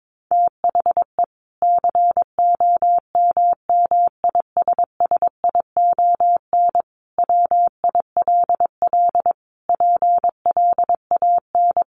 Morse code, heard as a keyed tone, about 22 wpm, 700 Hz. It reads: THE COMMISSION WILL PLAN
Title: Sentences from Top 1400 Words 22wpm